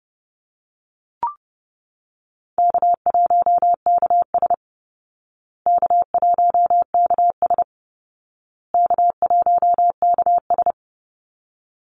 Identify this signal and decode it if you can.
Morse code, heard as a keyed tone, about 30 wpm, 700 Hz. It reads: K1KS K1KS K1KS
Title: Easy Call Signs Repeated 3x - 30wpm